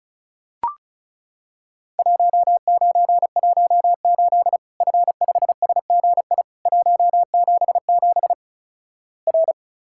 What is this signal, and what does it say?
Morse code, heard as a keyed tone, about 36 wpm, 700 Hz. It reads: 1918 F5SGI 177 R
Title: Worked All Europe DX Contest - QTC Exchange - 35wpm